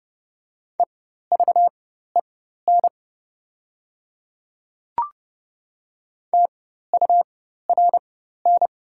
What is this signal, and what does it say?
Morse code, heard as a keyed tone, about 29 wpm, spaced as for 14 wpm, 690 Hz. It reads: EVEN TURN
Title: 4x Character Spacing - Top 200 Words - 30wpm